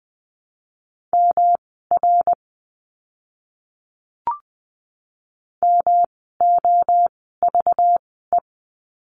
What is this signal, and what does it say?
Morse code, heard as a keyed tone, about 20 wpm, spaced as for 14 wpm, 700 Hz. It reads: MR MOVE
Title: 244 Lesson 39f - Letter z - Words 20wpm